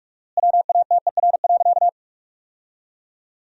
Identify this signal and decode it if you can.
Morse code, heard as a keyed tone, about 46 wpm, 700 Hz. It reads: WATER.
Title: 4x Word-Spacing - Sentences from Top 300 Words - 45wpm